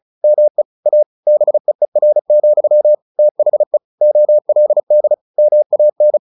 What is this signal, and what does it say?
Morse code, heard as a keyed tone, about 37 wpm, 600 Hz. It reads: ME A BEER, THE OLD MAN
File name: the old man and the sea / the old man and the sea 1 35wpm